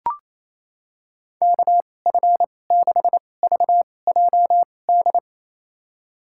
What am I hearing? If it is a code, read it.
Morse code, heard as a keyed tone, about 28 wpm, spaced as for 20 wpm, 700 Hz. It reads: KF6VJD
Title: USA Call Signs - 6 Characters - 2x Character Spacing - 28wpm